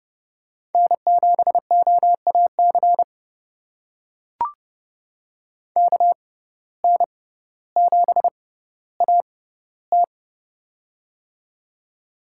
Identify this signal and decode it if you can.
Morse code, heard as a keyed tone, about 30 wpm, spaced as for 15 wpm, 700 Hz. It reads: N7OAC KN7AT
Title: USA Call Signs - 5 Characters - 6x Character Spacing - 30wpm